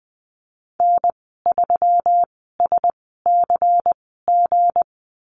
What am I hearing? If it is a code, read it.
Morse code, heard as a keyed tone, about 20 wpm, spaced as for 15 wpm, 700 Hz. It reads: N3SCG
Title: 167 Lesson 27g - Number 7 - Call Signs 20wpm